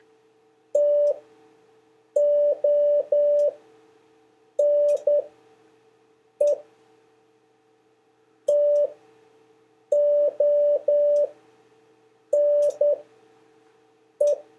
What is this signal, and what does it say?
Morse code, heard as a keyed tone, about 10 wpm, spaced as for 6 wpm, 580 Hz. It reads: TONE TONE
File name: 4of5-words-3